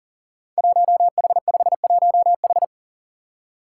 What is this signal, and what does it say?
Morse code, heard as a keyed tone, about 43 wpm, 700 Hz. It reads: 1H51H
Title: Five Challenging Letter-Number Combinations 40wpm